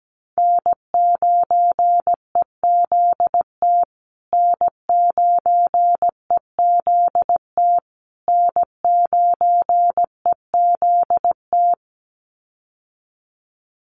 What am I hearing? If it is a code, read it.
Morse code, heard as a keyed tone, about 17 wpm, 700 Hz. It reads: N9EZT N9EZT N9EZT
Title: USA Call Signs Repeated 3x 17wpm